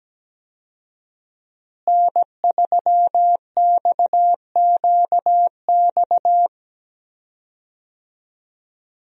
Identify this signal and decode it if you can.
Morse code, heard as a keyed tone, about 17 wpm, 700 Hz. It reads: N3XQX
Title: USA Call Signs - 5 Characters - Repeated 2x - 17wpm